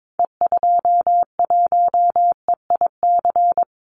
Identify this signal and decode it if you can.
Morse code, heard as a keyed tone, about 22 wpm, 700 Hz. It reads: E21EIC
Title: Call Signs - 22wpm